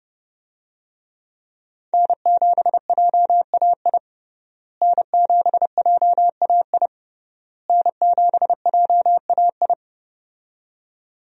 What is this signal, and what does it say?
Morse code, heard as a keyed tone, about 31 wpm, 700 Hz. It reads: N7JAI N7JAI N7JAI
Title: USA Call Signs - 5 Characters - Repeated 3x - 30wpm